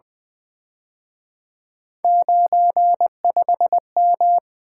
Morse code, heard as a keyed tone, about 24 wpm, 700 Hz. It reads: 95M
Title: Speed-Racing - Three Letter-Number Combinations - 20wpm